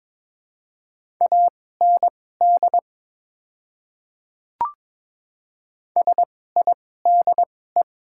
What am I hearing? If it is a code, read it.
Morse code, heard as a keyed tone, about 22 wpm, spaced as for 16 wpm, 700 Hz. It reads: AND SIDE
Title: 2x Character Spacing - Top 100 Words - 22wpm